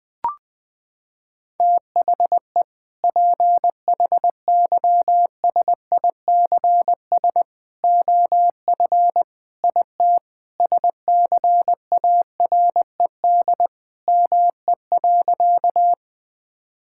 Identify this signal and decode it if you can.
Morse code, heard as a keyed tone, about 20 wpm, 700 Hz. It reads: THE PHYSICS OF IT SCARED ME.
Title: Sentences from Top 1000 Words - Review Part 5 - 20wpm